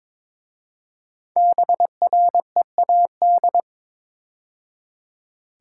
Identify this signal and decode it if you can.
Morse code, heard as a keyed tone, about 22 wpm, 700 Hz. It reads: BREAD
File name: 5 Letter Words - First 3 Letters Given - 22wpm